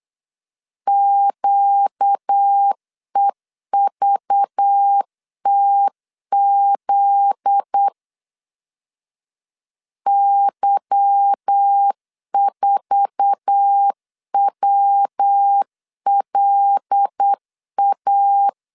Morse code, heard as a keyed tone, about 8 wpm, 800 Hz. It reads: QEVTZ Y4WLA
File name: LEZIONI / 8 YJ45 / 8 YJ4 8wpm 3246